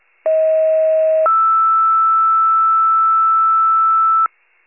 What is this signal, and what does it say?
Two-tone page, heard as a one-second tone, then a three-second tone, about 640 then 1320 Hz. It is heard from 0.3 to 4.3 s.